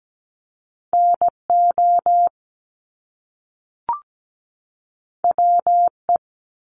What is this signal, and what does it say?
Morse code, heard as a keyed tone, about 17 wpm, 700 Hz. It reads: NO WE
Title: Top Two Letter Combinations in Words 17wpm